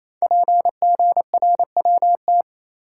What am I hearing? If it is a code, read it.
Morse code, heard as a keyed tone, about 27 wpm, 700 Hz. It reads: PGRWT
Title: Code Groups - 5 Letters - 28wpm